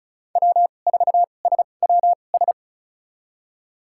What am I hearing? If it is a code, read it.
Morse code, heard as a keyed tone, about 36 wpm, spaced as for 25 wpm, 700 Hz. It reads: W4SWS
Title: USA Call Signs - 5 Characters - 2x Character Spacing - 35wpm